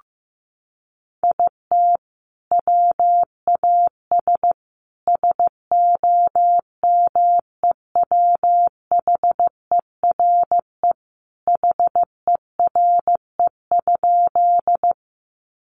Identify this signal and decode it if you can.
Morse code, heard as a keyed tone, about 15 wpm, 700 Hz. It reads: IT WAS SOMEWHERE HERE?
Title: Sentences from Top 1400 Words 15wpm